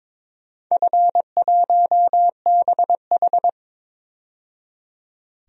Morse code, heard as a keyed tone, about 23 wpm, 700 Hz. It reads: F1BH
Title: DXCC-100 Call Signs - 4 Characters - 22wpm